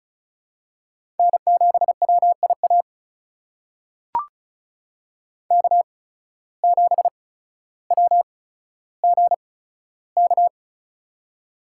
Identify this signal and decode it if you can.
Morse code, heard as a keyed tone, about 36 wpm, spaced as for 15 wpm, 700 Hz. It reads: N7WIA K7WGK
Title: USA Call Signs - 5 Characters - 8x Character Spacing - 35wpm